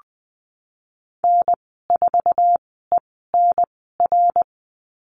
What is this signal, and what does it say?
Morse code, heard as a keyed tone, about 20 wpm, spaced as for 15 wpm, 700 Hz. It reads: N4ENR
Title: 055 Lesson 10g - Letter r - Call Signs 20wpm